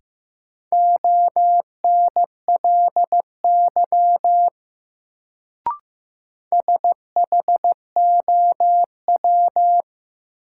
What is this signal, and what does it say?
Morse code, heard as a keyed tone, about 15 wpm, 700 Hz. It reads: ONLY SHOW
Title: Sentences from Top 200 Words - New Word Drill - 15wpm